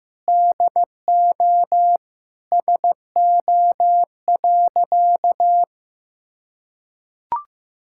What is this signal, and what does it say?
Morse code, heard as a keyed tone, about 14 wpm, 700 Hz. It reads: DO SO.
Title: Sentences from Top 1000 Words - Review Part 5 - 15wpm